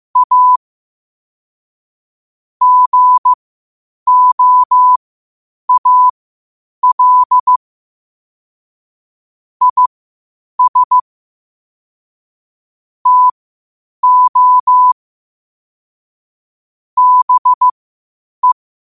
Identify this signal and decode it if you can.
Morse code, heard as a keyed tone, about 15 wpm, spaced as for 8 wpm, 1000 Hz. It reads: A GOAL IS TO BE
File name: qotdfiles / 8wpm